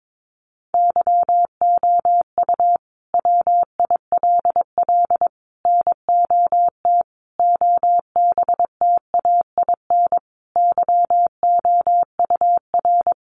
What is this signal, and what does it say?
Morse code, heard as a keyed tone, about 22 wpm, 700 Hz. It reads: YOU WILL NOT OBTAIN YOUR
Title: Sentences from Top 1300 Words 22wpm